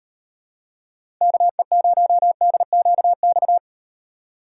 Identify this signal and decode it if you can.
Morse code, heard as a keyed tone, about 38 wpm, 700 Hz. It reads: KE0DQX